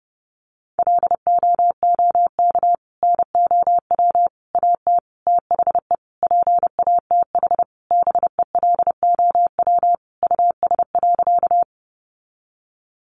Morse code, heard as a keyed tone, about 30 wpm, 700 Hz. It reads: LOOK NOW AT THE PATH BELOW US.
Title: Sentences from Top 1400 Words 30wpm